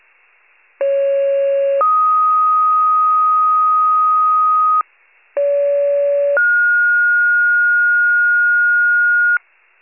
Two-tone page, heard as a one-second tone, then a three-second tone, about 550 then 1250 Hz, about 570 then 1430 Hz, with noise. It is heard from 0.8 to 9.4 s.
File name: pager / 29 FIRE / wcalert